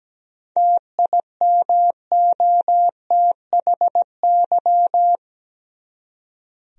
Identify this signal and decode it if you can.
Morse code, heard as a keyed tone, about 17 wpm, 700 Hz. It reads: TIMOTHY
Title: USA Common Names - 17wpm